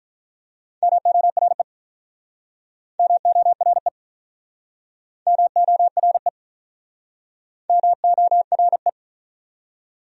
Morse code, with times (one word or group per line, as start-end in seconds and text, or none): 0.8-1.6 MORE
3.0-3.9 MORE
5.3-6.3 MORE
7.7-8.9 MORE